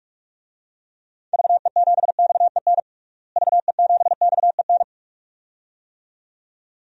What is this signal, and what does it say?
Morse code, heard as a keyed tone, about 46 wpm, 700 Hz. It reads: VE7XEN VE7XEN
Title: DXCC-100 Call Signs - 6 Characters - Repeated 2x - 45wpm